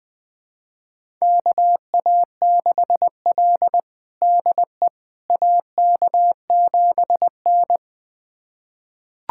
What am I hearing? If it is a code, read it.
Morse code, heard as a keyed tone, about 20 wpm, 700 Hz. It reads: KA6L DE AK7N
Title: USA Callsign DE Callsign - 4 Characters - 20wpm